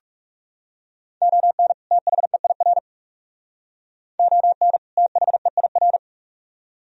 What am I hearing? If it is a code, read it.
Morse code, heard as a keyed tone, about 44 wpm, 700 Hz. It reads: ON THEIR ON THEIR